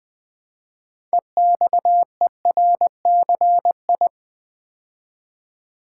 Morse code, heard as a keyed tone, about 20 wpm, 700 Hz. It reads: EXERCI
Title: Build A Word - 2 Letters At A Time - 8-Letter Words - 20wpm